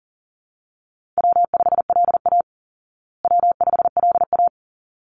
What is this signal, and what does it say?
Morse code, heard as a keyed tone, about 40 wpm, 700 Hz. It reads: W5LA W5LA
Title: Call Signs Repeated 3x - 40wpm